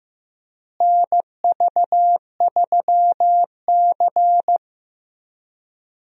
Morse code, heard as a keyed tone, about 16 wpm, 700 Hz. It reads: NV3C
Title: USA Call Signs - 4 Characters - Repeated 2x - 15wpm